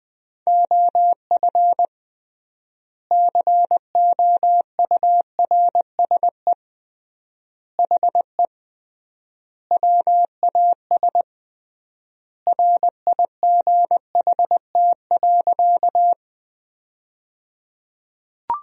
Morse code, heard as a keyed tone, about 20 wpm, 700 Hz. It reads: OF COURSE HE WAS RIGHT.
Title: 3x Word-Spacing - Sentences from Top 500 Words - 20wpm